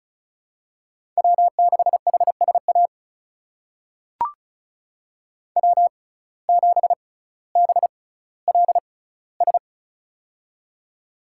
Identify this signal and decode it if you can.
Morse code, heard as a keyed tone, about 36 wpm, spaced as for 18 wpm, 700 Hz. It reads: W6HSA W7BLS
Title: USA Call Signs - 5 Characters - 6x Character Spacing - 35wpm